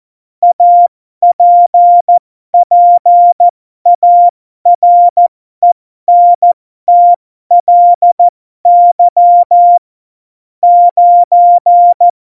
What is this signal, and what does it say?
Morse code, heard as a keyed tone, about 14 wpm, spaced as for 12 wpm, 700 Hz. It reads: APPARENTLY 9
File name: Morsekurs / Witze